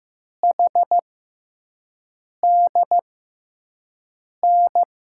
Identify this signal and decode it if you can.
Morse code, heard as a keyed tone, about 15 wpm, 700 Hz. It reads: H D N
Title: USA Call Signs - 6x Character Spacing - 15wpm